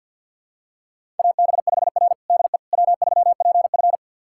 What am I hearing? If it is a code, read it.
Morse code, heard as a keyed tone, about 51 wpm, 700 Hz. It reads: AB5R DE W3PF
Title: USA Callsign DE Callsign - 4 Characters - 50wpm